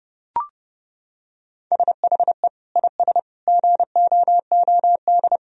Morse code, heard as a keyed tone, about 30 wpm, 700 Hz. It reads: SHE IS GOOD
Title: Sentences from Top 1000 Words - Review Part 3 - 30wpm